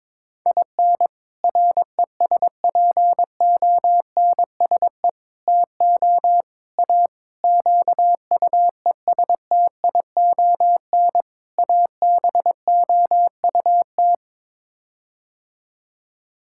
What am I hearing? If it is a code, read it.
Morse code, heard as a keyed tone, about 22 wpm, 700 Hz. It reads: IN RESPONSE TO A QUESTION ABOUT
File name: practice / 22wpm / Sets of 6 Words - 22wpm